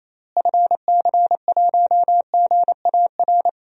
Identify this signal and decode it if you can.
Morse code, heard as a keyed tone, about 28 wpm, 700 Hz. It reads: FC1GAR
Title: DXCC-100 Call Signs - 6 Characters - Repeated 3x - 28wpm